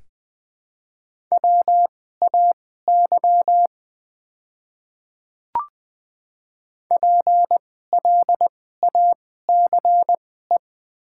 Morse code, heard as a keyed tone, about 20 wpm, spaced as for 14 wpm, 700 Hz. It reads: WAY PLACE
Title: 2x Character Spacing - Top 500 Words - 20wpm